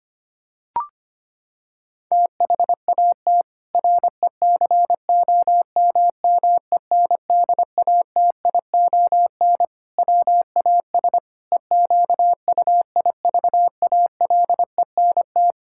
Morse code, heard as a keyed tone, about 25 wpm, 700 Hz. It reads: THAT RECOMMENDATION WAS EQUIVALENT
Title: Sentences from Top 1600 Words 25wpm